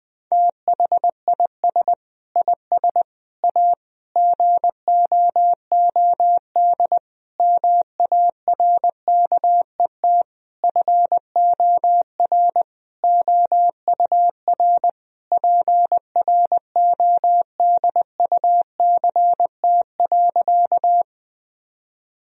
Morse code, heard as a keyed tone, about 20 wpm, 700 Hz. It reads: THIS IS A GOOD MARKET FOR OUR PRODUCT.
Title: Sentences from Top 1000 Words - Review Part 3 - 20wpm